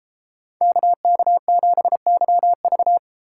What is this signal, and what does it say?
Morse code, heard as a keyed tone, about 32 wpm, 700 Hz. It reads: KK7YV